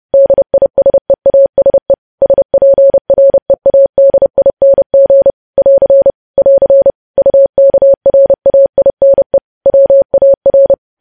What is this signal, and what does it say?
Morse code, heard as a keyed tone, about 30 wpm, 550 Hz. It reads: DISEASE SPREADING <AR> <AR> UKRAINE WAR
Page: news headlines in morse code at 30 wpm